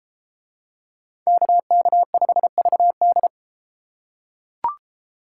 Morse code, heard as a keyed tone, about 31 wpm, 700 Hz. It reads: KK5VD